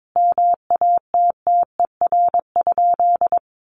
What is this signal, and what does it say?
Morse code, heard as a keyed tone, about 22 wpm, 700 Hz. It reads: MATTER?
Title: Sentences from Top 1000 Words 22wpm